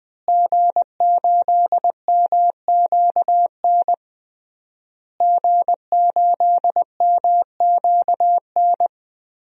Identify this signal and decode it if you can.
Morse code, heard as a keyed tone, about 21 wpm, 700 Hz. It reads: G8MQN G8MQN
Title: DXCC-100 Call Signs - 5 Characters - Repeated 3x - 20wpm